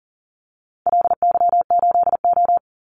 Morse code, heard as a keyed tone, about 40 wpm, 700 Hz. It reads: LY8O
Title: Call Signs - 40wpm